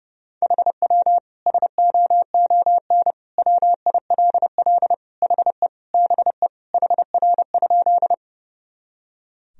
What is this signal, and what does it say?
Morse code, heard as a keyed tone, about 30 wpm, 700 Hz. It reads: HW SOON WILL HE BE HR?
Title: Sentences from Top 400 Words and Abbreviations 30wpm